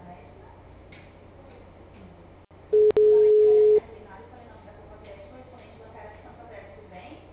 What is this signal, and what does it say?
Ringback tone (Europe, 425 Hz), with speech.
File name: out- -2018- - - .89045